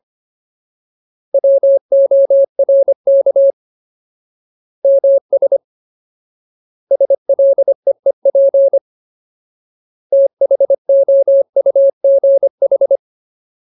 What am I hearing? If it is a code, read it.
Morse code, heard as a keyed tone, about 25 wpm, 550 Hz. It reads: WORK MS SLEEP THOUGH
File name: Ultimate Word List / 25WPM Ultimate